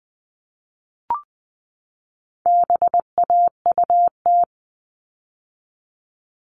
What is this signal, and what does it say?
Morse code, heard as a keyed tone, about 20 wpm, 700 Hz. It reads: BAUT
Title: All Cut Numbers - 4-Digit 20wpm